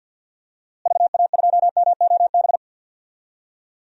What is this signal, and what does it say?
Morse code, heard as a keyed tone, about 47 wpm, 700 Hz. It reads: VA2MOB
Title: DXCC-100 Call Signs - 6 Characters - Repeated 3x - 50wpm